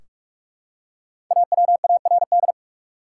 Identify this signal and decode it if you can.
Morse code, heard as a keyed tone, about 45 wpm, 700 Hz. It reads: AWARD
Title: Five Letter Words - 45wpm